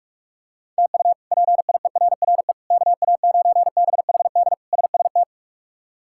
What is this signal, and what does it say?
Morse code, heard as a keyed tone, about 46 wpm, 700 Hz. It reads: TU PIERRE KA9BHD SST
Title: K1USN SST QSO Practice - 45wpm